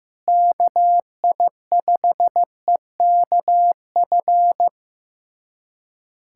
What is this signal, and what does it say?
Morse code, heard as a keyed tone, about 15 wpm, 700 Hz. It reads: KI5EKF
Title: USA Callsign DE Callsign 15wpm